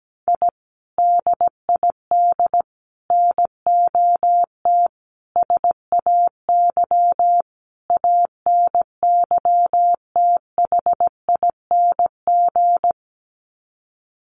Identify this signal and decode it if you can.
Morse code, heard as a keyed tone, about 17 wpm, 700 Hz. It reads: I DID NOT SAY ANYTHING
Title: Sets of 5 Words Encore2 17wpm